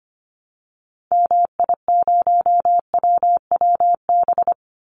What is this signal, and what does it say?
Morse code, heard as a keyed tone, about 25 wpm, 700 Hz. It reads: MI0WWB
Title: Call Signs Repeated 3x - 25wpm